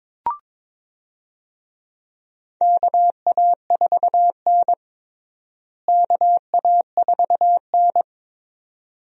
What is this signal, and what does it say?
Morse code, heard as a keyed tone, about 22 wpm, 700 Hz. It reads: KA4N KA4N
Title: USA Call Signs - 4 Characters - Repeated 2x - 22wpm